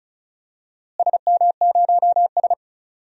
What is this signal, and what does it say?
Morse code, heard as a keyed tone, about 36 wpm, 700 Hz. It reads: SM0S
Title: DXCC-100 Call Signs - 4 Characters - Repeated 2x - 35wpm